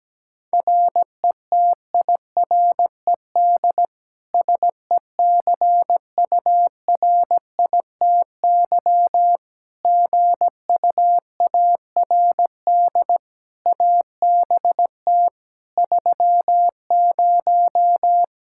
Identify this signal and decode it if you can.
Morse code, heard as a keyed tone, about 17 wpm, 700 Hz. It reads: RETIRED SECURITY GUARD ABT 30
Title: K1USN SST QSO Practice 17wpm